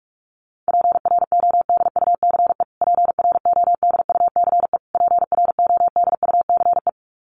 Morse code, heard as a keyed tone, about 44 wpm, 700 Hz. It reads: PRODUCE PRODUCE PRODUCE
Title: Top 500 Words Repeated 45wpm